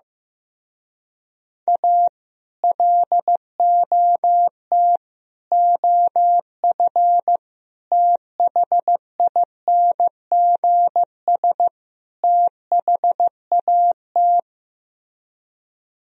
Morse code, heard as a keyed tone, about 15 wpm, 700 Hz. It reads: A LOT OF THINGS THAT
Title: Speed-Racing - Sets of 5 Words 15wpm